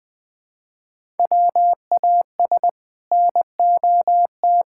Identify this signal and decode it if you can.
Morse code, heard as a keyed tone, about 21 wpm, 700 Hz. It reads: WAS NOT
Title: Build A Phrase - 2-Word Phrases - 20wpm